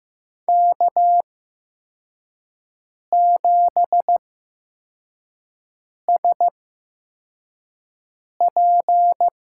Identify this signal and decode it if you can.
Morse code, heard as a keyed tone, about 15 wpm, spaced as for 4 wpm, 700 Hz. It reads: K7SP
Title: USA Call Signs - 5 Characters - 8x Character Spacing - 15wpm